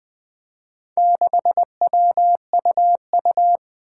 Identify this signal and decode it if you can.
Morse code, heard as a keyed tone, about 20 wpm, 700 Hz. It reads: 6WUU
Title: Four Challenging Letter-Number Combinations 20wpm